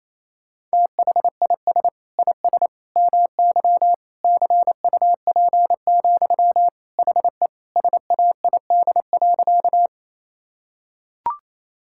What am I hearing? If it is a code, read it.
Morse code, heard as a keyed tone, about 28 wpm, 700 Hz. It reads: THIS IS MY CUP, HE SAID.
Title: Sentences from Top 1600 Words 28wpm